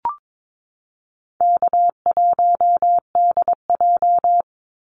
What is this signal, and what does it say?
Morse code, heard as a keyed tone, about 22 wpm, 700 Hz. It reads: K1DJ
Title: Easy Call Signs Repeated 2x - 22wpm